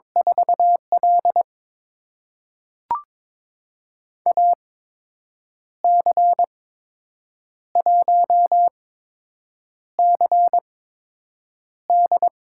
Morse code, heard as a keyed tone, about 22 wpm, spaced as for 7 wpm, 700 Hz. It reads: 4L AC1CD
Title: USA Call Signs - 8x Character Spacing - 22wpm